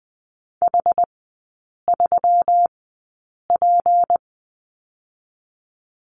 Morse code, heard as a keyed tone, about 20 wpm, spaced as for 8 wpm, 700 Hz. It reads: H3P
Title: 205 Lesson 33e - Letter j - Three Characters 20wpm